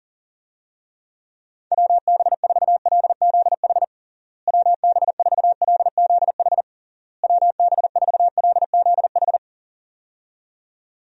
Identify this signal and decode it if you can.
Morse code, heard as a keyed tone, about 40 wpm, 700 Hz. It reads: WB4LZH WB4LZH WB4LZH
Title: USA Call Signs - 6 Characters - Repeated 3x - 40wpm